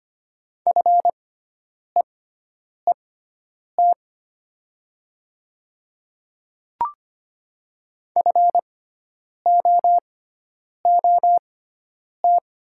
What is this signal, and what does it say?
Morse code, heard as a keyed tone, about 25 wpm, spaced as for 9 wpm, 700 Hz. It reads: FEET FOOT
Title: 6x Character Spacing - Top 500 Words - 25wpm